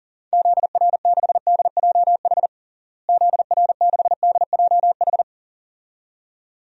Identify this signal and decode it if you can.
Morse code, heard as a keyed tone, about 40 wpm, 700 Hz. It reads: ZR6DJH ZR6DJH
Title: DXCC-100 Call Signs - 6 Characters - Repeated 2x - 40wpm